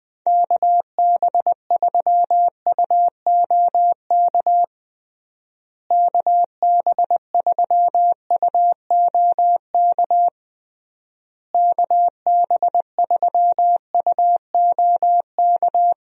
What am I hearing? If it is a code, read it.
Morse code, heard as a keyed tone, about 20 wpm, 700 Hz. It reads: KB3UOK KB3UOK KB3UOK
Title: USA Call Signs - 6 Characters - Repeated 3x - 20wpm